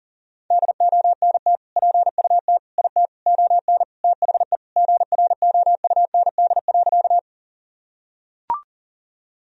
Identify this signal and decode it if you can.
Morse code, heard as a keyed tone, about 40 wpm, 700 Hz. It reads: DONT PUT IT ON THE GROUND.
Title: Sentences from Top 1000 Words - Review Part 2 - 40wpm